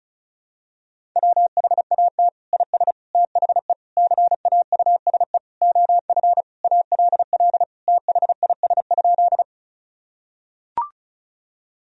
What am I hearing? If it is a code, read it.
Morse code, heard as a keyed tone, about 36 wpm, 700 Hz. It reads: WHAT IS THE CAUSE OF ALL THIS?
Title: Sentences from Top 1000 Words - Review Part 1 - 35wpm